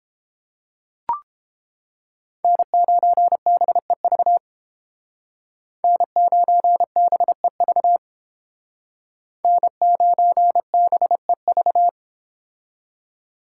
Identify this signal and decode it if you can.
Morse code, heard as a keyed tone, about 30 wpm, 700 Hz. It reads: N9BEV N9BEV N9BEV